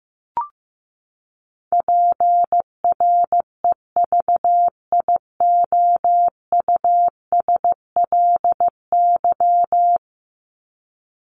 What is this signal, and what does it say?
Morse code, heard as a keyed tone, about 15 wpm, 700 Hz. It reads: PREVIOUSLY
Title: LY Suffix - Top 100 Words with Base Words 15wpm